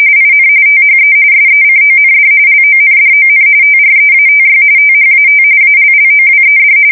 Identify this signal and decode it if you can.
RTTY, about 45.45 baud. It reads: IN THE SECOND LONDON TREATY THAT ALLOWED